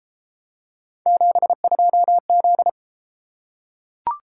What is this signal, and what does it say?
Morse code, heard as a keyed tone, about 29 wpm, 700 Hz. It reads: 72Z